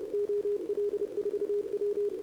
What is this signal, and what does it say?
Morse code, heard as a keyed tone, about 31 wpm, 410 Hz, with radio noise. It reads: OL4W
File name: CQWWCW2025 / NedeleOdpol 20m Scan